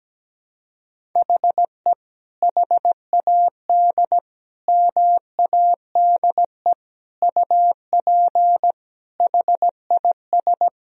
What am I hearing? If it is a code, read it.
Morse code, heard as a keyed tone, about 17 wpm, 700 Hz. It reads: HE HAD MADE UP HIS
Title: Sentences from Top 1000 Words - Review Part 3 - 17wpm